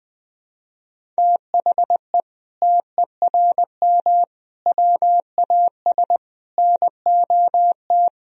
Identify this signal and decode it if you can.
Morse code, heard as a keyed tone, about 20 wpm, 700 Hz. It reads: THE TERM WAS NOT
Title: Sentences from Top 1000 Words - Review Part 4 - 20wpm